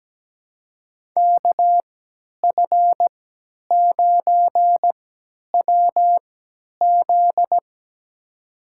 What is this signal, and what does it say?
Morse code, heard as a keyed tone, about 17 wpm, spaced as for 10 wpm, 700 Hz. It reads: KF9WZ
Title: USA Call Signs - 5 Characters - 3x Character Spacing - 17wpm